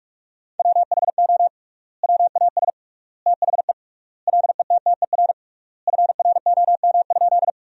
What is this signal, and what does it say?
Morse code, heard as a keyed tone, about 46 wpm, 700 Hz. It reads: WHO WAS THE LETTER FROM?
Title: 3x Word-Spacing - Sentences from Top 300 Words - 45wpm